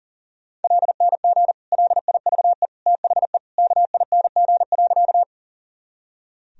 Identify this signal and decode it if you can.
Morse code, heard as a keyed tone, about 40 wpm, 700 Hz. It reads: LNG LIVE THE KING.
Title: Sentences from Top 400 Words and Abbreviations 40wpm